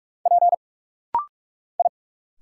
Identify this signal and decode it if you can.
Morse code, heard as a keyed tone, about 46 wpm, 700 Hz. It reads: P I
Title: Single Letter-Number - Rapid-Fire 45wpm